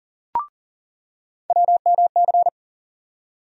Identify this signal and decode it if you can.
Morse code, heard as a keyed tone, about 38 wpm, 700 Hz. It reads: WMC
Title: USA Call Sign Postfixes - 3 Letters 40wpm